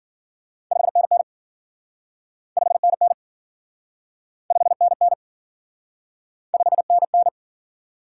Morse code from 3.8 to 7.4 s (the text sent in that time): 5NN 5NN